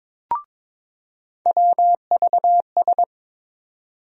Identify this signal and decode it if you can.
Morse code, heard as a keyed tone, about 22 wpm, 700 Hz. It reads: WVS
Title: USA Call Sign Postfixes - 3 Letters 22wpm